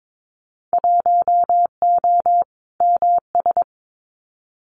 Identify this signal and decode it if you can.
Morse code, heard as a keyed tone, about 22 wpm, 700 Hz. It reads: 1O MS
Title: Winter Field Day Exchanges 22wpm